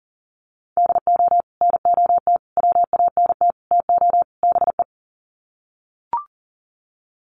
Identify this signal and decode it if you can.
Morse code, heard as a keyed tone, about 40 wpm, 700 Hz. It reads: DO NOT WANT TO BE
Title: Sets of 5 Words 40wpm